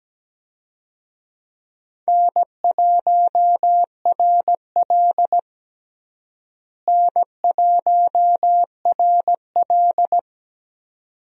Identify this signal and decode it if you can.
Morse code, heard as a keyed tone, about 17 wpm, 700 Hz. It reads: N1RL N1RL
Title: USA Call Signs - 4 Characters - Repeated 2x - 17wpm